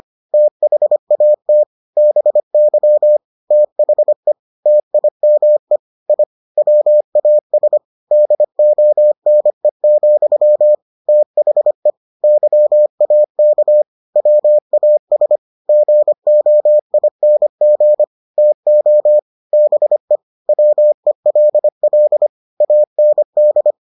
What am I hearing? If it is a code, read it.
Morse code, heard as a keyed tone, about 25 wpm, 600 Hz. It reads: THAT BY THE TIME I WAS DONE, THE YAK WAS GOING TO BE WELL AND